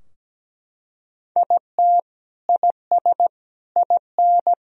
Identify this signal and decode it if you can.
Morse code, heard as a keyed tone, about 17 wpm, 700 Hz. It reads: IT IS IN
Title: Sentences from Top 1000 Words - Review Part 4 - 17wpm